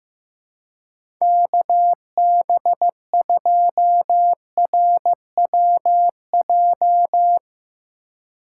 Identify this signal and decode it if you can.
Morse code, heard as a keyed tone, about 15 wpm, 700 Hz. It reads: KB2RWJ
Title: USA Call Signs - 6 Characters - 4x Character Spacing - 15wpm